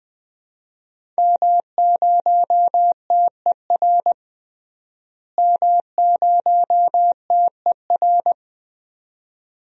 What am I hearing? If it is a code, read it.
Morse code, heard as a keyed tone, about 20 wpm, 700 Hz. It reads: M0TER M0TER
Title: DXCC-100 Call Signs - 5 Characters - Repeated 2x - 20wpm